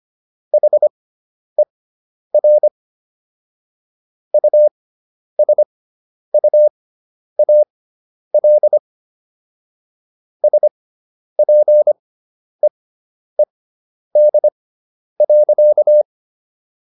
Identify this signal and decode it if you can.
Morse code, heard as a keyed tone, about 25 wpm, spaced as for 10 wpm, 600 Hz. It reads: HER USUAL SPEED.